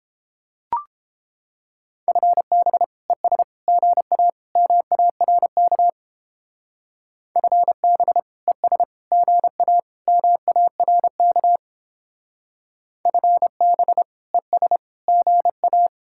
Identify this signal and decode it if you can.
Morse code, heard as a keyed tone, about 29 wpm, 700 Hz. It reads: FB ES GA MARK FB ES GA MARK FB ES GA